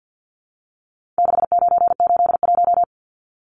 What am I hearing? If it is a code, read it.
Morse code, heard as a keyed tone, about 50 wpm, 700 Hz. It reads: 6981
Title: Numbers - 4-Digit 50wpm